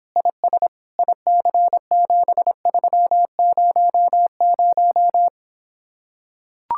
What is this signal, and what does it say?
Morse code, heard as a keyed tone, about 26 wpm, 700 Hz. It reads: IS IC7300